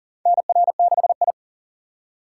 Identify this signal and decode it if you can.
Morse code, heard as a keyed tone, about 40 wpm, 700 Hz. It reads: NR6I
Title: USA Call Signs - 4 Characters - 6x Character Spacing - 40wpm